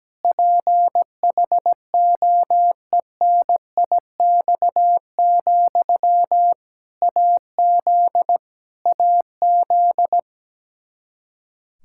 Morse code, heard as a keyed tone, about 17 wpm, 700 Hz. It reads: PHOENIX, AZ AZ
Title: K1USN SST QSO Practice 17wpm